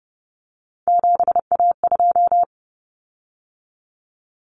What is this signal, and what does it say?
Morse code, heard as a keyed tone, about 30 wpm, 700 Hz. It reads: 7A2
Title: Common Cut Numbers - 3-Digit 30wpm